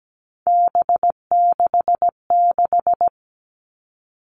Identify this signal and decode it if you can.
Morse code, heard as a keyed tone, about 17 wpm, 700 Hz. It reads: B66
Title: All Cut Numbers - 3-Digit 17wpm